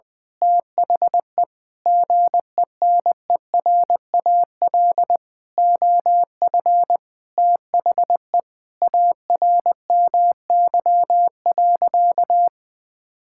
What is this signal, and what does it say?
Morse code, heard as a keyed tone, about 20 wpm, 700 Hz. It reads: THE GENERAL OF THE ARMY.
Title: Sentences from Top 1600 Words 20wpm